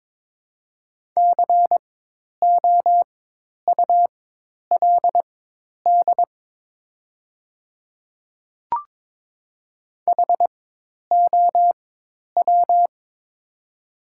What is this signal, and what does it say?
Morse code, heard as a keyed tone, about 22 wpm, spaced as for 10 wpm, 700 Hz. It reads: COULD HOW
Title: 4x Character Spacing - Top 200 Words - 22wpm